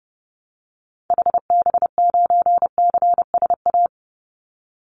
Morse code, heard as a keyed tone, about 30 wpm, 700 Hz. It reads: HB9CSA
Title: Call Signs - 30wpm